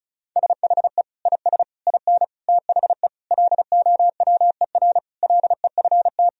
Morse code, heard as a keyed tone, about 36 wpm, 700 Hz. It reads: SHE IS IN THE LOWER LEFT
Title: Sentences from Top 1000 Words - Review Part 4 - 35wpm